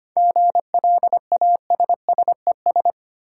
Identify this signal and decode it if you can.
Morse code, heard as a keyed tone, about 25 wpm, 700 Hz. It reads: GLASSES
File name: practice / 25wpm / Easy Words with Hints - 25wpm